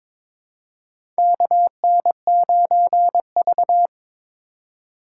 Morse code, heard as a keyed tone, about 23 wpm, 700 Hz. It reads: KN9V
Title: USA Call Signs - 3x Character Spacing - 22wpm